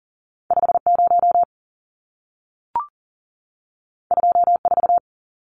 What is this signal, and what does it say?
Morse code, heard as a keyed tone, about 40 wpm, 700 Hz. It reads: 50 24
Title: Numbers - 2-Digit 40wpm